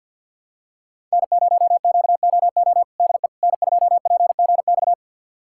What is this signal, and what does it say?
Morse code, heard as a keyed tone, about 52 wpm, 700 Hz. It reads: N0QOO DE N2PGX
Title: USA Callsign DE Callsign - 5 Characters - 50wpm